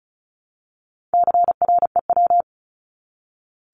Morse code, heard as a keyed tone, about 34 wpm, 700 Hz. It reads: CREW
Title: Top 1251-1500 Words 35wpm